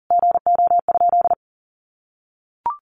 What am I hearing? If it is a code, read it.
Morse code, heard as a keyed tone, about 40 wpm, 700 Hz. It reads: GO?
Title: Sentences from Top 1300 Words 40wpm